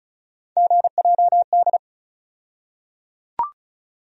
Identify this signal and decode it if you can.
Morse code, heard as a keyed tone, about 36 wpm, 700 Hz. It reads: GJD
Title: USA Call Sign Postfixes - 3 Letters 35wpm